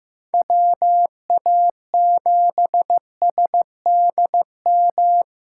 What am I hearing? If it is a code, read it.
Morse code, heard as a keyed tone, about 15 wpm, 700 Hz. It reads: WA7SDM
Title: USA Callsign DE Callsign - 6 Characters - 15wpm